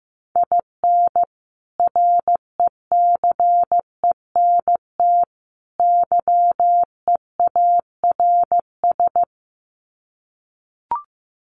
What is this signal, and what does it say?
Morse code, heard as a keyed tone, about 15 wpm, 700 Hz. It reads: IN RECENT YEARS
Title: Sets of 3 Words 15wpm